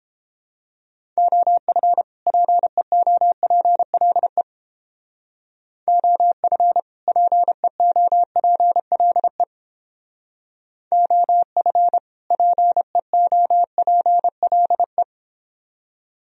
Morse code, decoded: OF PEOPLE OF PEOPLE OF PEOPLE

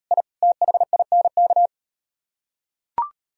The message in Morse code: I THINK

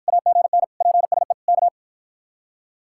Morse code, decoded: AGN PSE K

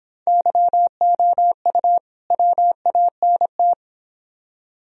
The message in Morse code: YOU WANT

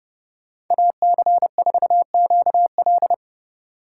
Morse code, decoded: AC4QL

